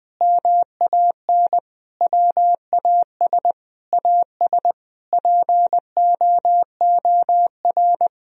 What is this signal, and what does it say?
Morse code, heard as a keyed tone, about 21 wpm, 700 Hz. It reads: MAN WAS AS POOR
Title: Sentences from Top 1000 Words - Review Part 2 - 20wpm